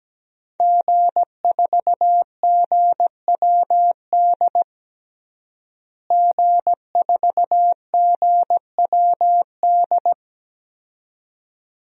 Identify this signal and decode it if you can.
Morse code, heard as a keyed tone, about 17 wpm, 700 Hz. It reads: G4GWD G4GWD
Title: DXCC-100 Call Signs - 5 Characters - Repeated 2x - 17wpm